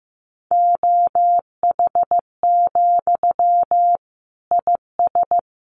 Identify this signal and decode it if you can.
Morse code, heard as a keyed tone, about 14 wpm, 700 Hz. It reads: OH, IS
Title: Sentences from Top 800 Words 15wpm